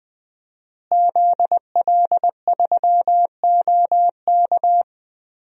Morse code, heard as a keyed tone, about 20 wpm, 700 Hz. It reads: ZL3OK
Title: DXCC-100 Call Signs - 5 Characters - Repeated 2x - 20wpm